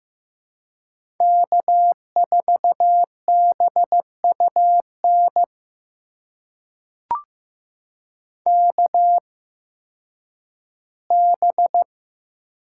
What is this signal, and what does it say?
Morse code, heard as a keyed tone, about 15 wpm, spaced as for 10 wpm, 700 Hz. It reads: K4BUN KB
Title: USA Call Signs - 8x Character Spacing - 15wpm